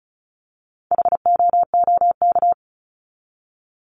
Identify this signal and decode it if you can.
Morse code, heard as a keyed tone, about 34 wpm, 700 Hz. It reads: HOOK
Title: Top 1501-1750 Words 35wpm